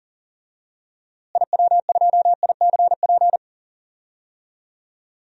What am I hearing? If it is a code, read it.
Morse code, heard as a keyed tone, about 40 wpm, 700 Hz. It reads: IW2ICP
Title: DXCC-100 Call Signs - 6 Characters - Repeated 2x - 40wpm